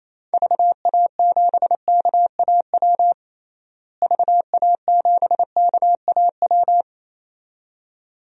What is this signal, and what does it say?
Morse code, heard as a keyed tone, about 28 wpm, 700 Hz. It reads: VA7KAW VA7KAW
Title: DXCC-100 Call Signs - 6 Characters - Repeated 2x - 28wpm